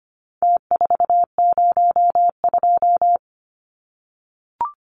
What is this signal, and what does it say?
Morse code, heard as a keyed tone, about 25 wpm, 700 Hz. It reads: T402
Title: Numbers - Contest Style - 4-Digit 25wpm